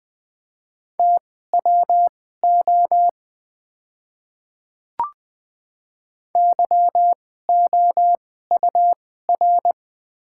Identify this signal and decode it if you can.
Morse code, heard as a keyed tone, about 21 wpm, spaced as for 15 wpm, 690 Hz. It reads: TWO YOUR
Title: 2x Character Spacing - Top 100 Words - 20wpm